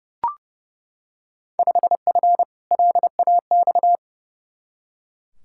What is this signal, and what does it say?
Morse code, heard as a keyed tone, about 30 wpm, 700 Hz. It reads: 5F LAX
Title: Field Day Exchanges Encore - 30wpm